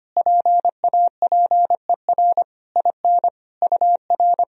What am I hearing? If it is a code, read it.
Morse code, heard as a keyed tone, about 25 wpm, 700 Hz. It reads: PAPER IN UR
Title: Sentences from Top 400 Words and Abbreviations 25wpm